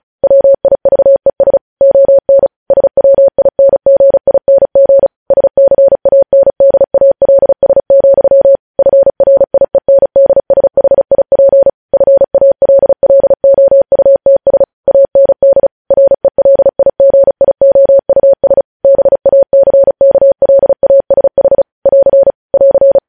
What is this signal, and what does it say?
Morse code, heard as a keyed tone, about 36 wpm, 550 Hz. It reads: WIVES ON SWINGING SCANDALS, FRIENDSHIP FALLOUTS AND RELIGIOUS BACKLASH <AR> <AR>
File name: podcast / News headlines-1x-35x35